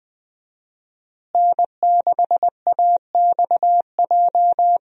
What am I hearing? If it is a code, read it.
Morse code, heard as a keyed tone, about 20 wpm, 700 Hz. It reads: N6AXJ
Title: USA Call Signs - 5 Characters - Repeated 2x - 20wpm